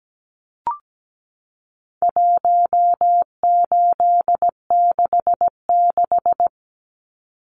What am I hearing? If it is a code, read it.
Morse code, heard as a keyed tone, about 17 wpm, 700 Hz. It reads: 1866
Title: Numbers - 4-Digit 17wpm